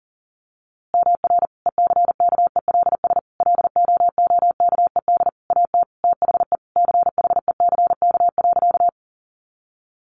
Morse code, heard as a keyed tone, about 40 wpm, 700 Hz. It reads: MR ECKELS LOOKED AT THE CHECK.
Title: A Sound Of Thunder 40wpm